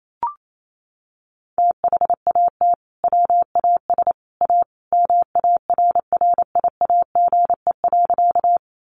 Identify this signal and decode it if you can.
Morse code, heard as a keyed tone, about 28 wpm, 700 Hz. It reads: THAT WAS A MARRIAGE.
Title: Sentences from Top 800 Words 28wpm